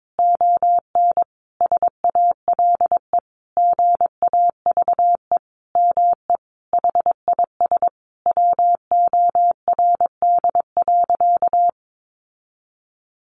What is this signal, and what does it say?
Morse code, heard as a keyed tone, about 22 wpm, 700 Hz. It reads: ON SALE GAVE ME HIS WORD.
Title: Sentences from Top 1400 Words 22wpm